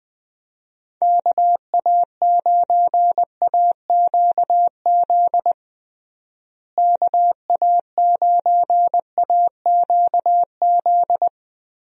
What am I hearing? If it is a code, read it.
Morse code, heard as a keyed tone, about 20 wpm, 700 Hz. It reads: KA9AQZ KA9AQZ
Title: USA Call Signs - 6 Characters - Repeated 3x - 20wpm